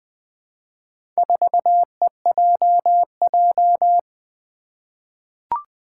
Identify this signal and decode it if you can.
Morse code, heard as a keyed tone, about 20 wpm, 700 Hz. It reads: 4EJJ
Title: Four Challenging Letter-Number Combinations 20wpm